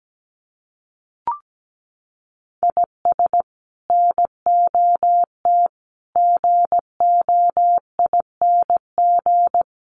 Morse code, heard as a keyed tone, about 17 wpm, 700 Hz. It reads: IS NOT GOING
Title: Sets of 4 Words 17wpm